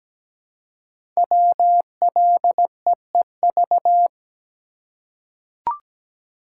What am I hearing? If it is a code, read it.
Morse code, heard as a keyed tone, about 17 wpm, 700 Hz. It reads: WLEEV
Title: Code Groups - 5 Letters - 17wpm